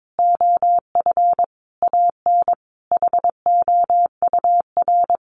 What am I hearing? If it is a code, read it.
Morse code, heard as a keyed tone, about 22 wpm, 700 Hz. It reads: OF AN HOUR
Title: Sentences from Top 1300 Words 22wpm